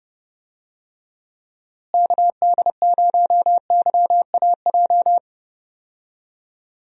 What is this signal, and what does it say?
Morse code, heard as a keyed tone, about 30 wpm, 700 Hz. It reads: KD0YAJ
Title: USA Call Signs - 6 Characters - Repeated 3x - 30wpm